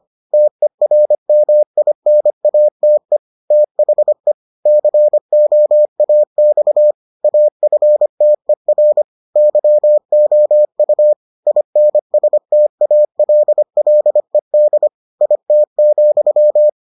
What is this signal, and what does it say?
Morse code, heard as a keyed tone, about 25 wpm, 600 Hz. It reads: TERMINATE THE COAX AFTER YOU INSTALLED IT,